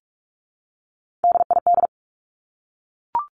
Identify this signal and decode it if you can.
Morse code, heard as a keyed tone, about 44 wpm, 700 Hz. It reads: DID
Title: Top 200 Words 45wpm